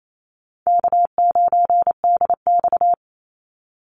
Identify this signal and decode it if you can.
Morse code, heard as a keyed tone, about 28 wpm, 700 Hz. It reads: K9DX
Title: Call Signs Repeated 3x - 28wpm